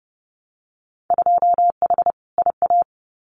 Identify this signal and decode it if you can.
Morse code, heard as a keyed tone, about 30 wpm, 700 Hz. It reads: 2H IA
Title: Winter Field Day Exchanges Encore 30wpm